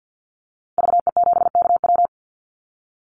Morse code, heard as a keyed tone, about 50 wpm, 700 Hz. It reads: VE7KW
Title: Call Signs - 50wpm